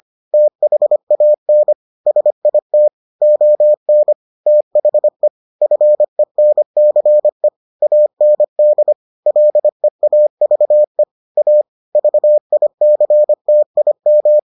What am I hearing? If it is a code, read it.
Morse code, heard as a keyed tone, about 25 wpm, 600 Hz. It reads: THAN SIT ON THE FENCE AND LEAVE A VICTIM